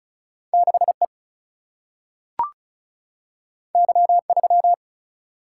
Morse code, heard as a keyed tone, about 36 wpm, 700 Hz. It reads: 6 E Y 3
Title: Single-Letter Single-Number 35wpm